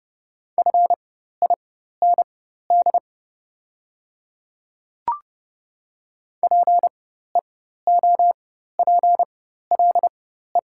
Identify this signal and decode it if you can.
Morse code, heard as a keyed tone, about 30 wpm, spaced as for 14 wpm, 700 Hz. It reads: FIND PEOPLE
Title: 4x Character Spacing - Top 100 Words - 30wpm